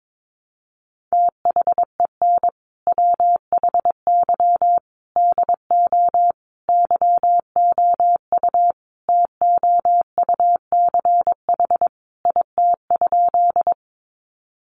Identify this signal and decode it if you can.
Morse code, heard as a keyed tone, about 22 wpm, 700 Hz. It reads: THEN WHY DO YOU TOUCH IT?
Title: Sentences from Top 1200 Words 22wpm